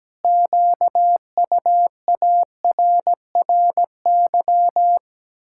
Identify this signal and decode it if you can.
Morse code, heard as a keyed tone, about 17 wpm, 700 Hz. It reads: QUARRY
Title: Pangram Challenge v2 - 17wpm